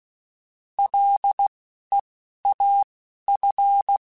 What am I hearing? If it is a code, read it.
Morse code, heard as a keyed tone, about 16 wpm, spaced as for 11 wpm, 800 Hz. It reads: LEAF